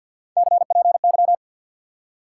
Morse code, heard as a keyed tone, about 50 wpm, 700 Hz. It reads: CPY
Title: Top 180 QSO Elements 50wpm